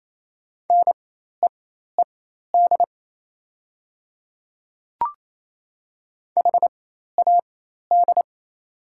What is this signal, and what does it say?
Morse code, heard as a keyed tone, about 28 wpm, spaced as for 13 wpm, 700 Hz. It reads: NEED HAD
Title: 4x Character Spacing - Top 200 Words - 28wpm